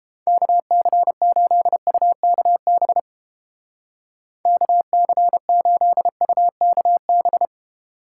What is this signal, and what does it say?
Morse code, heard as a keyed tone, about 29 wpm, 700 Hz. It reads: KC8UKB KC8UKB